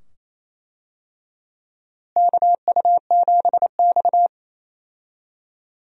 Morse code, heard as a keyed tone, about 27 wpm, 700 Hz. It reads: KU7X
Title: USA Call Signs - 4 Characters - Repeated 2x - 28wpm